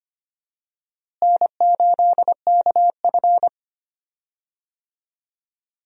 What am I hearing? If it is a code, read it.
Morse code, heard as a keyed tone, about 25 wpm, 700 Hz. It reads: N8KF
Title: USA Call Signs - 4 Characters - Repeated 2x - 25wpm